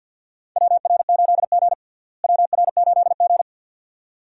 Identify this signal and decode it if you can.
Morse code, heard as a keyed tone, about 51 wpm, 700 Hz. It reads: WR8G WR8G
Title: USA Call Signs - 4 Characters - Repeated 2x - 50wpm